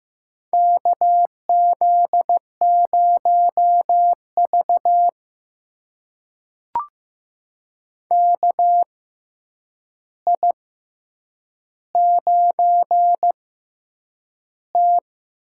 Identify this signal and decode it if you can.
Morse code, heard as a keyed tone, about 15 wpm, spaced as for 8 wpm, 700 Hz. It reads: KZ0V KI9T
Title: USA Call Signs - 4 Characters - 6x Character Spacing - 15wpm